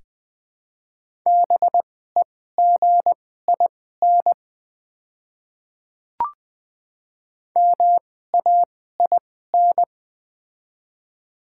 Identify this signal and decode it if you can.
Morse code, heard as a keyed tone, about 20 wpm, spaced as for 15 wpm, 700 Hz. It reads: BEGIN MAIN
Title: 2x Character Spacing - Top 300 Words - 20wpm